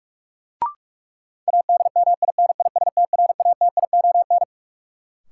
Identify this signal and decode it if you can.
Morse code, heard as a keyed tone, about 46 wpm, 700 Hz. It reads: ADMINISTRATION
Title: Sentences from Top 1300 Words - New Word Drill Alt - 45wpm